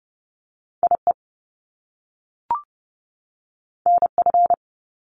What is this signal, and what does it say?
Morse code, heard as a keyed tone, about 30 wpm, 700 Hz. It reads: IE NF
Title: Top Two Letter Combinations in Words 30wpm